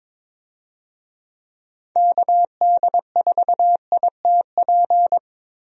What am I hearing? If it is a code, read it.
Morse code, heard as a keyed tone, about 22 wpm, 700 Hz. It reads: KD4ITP
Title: USA Call Signs - 6 Characters - Repeated 2x - 22wpm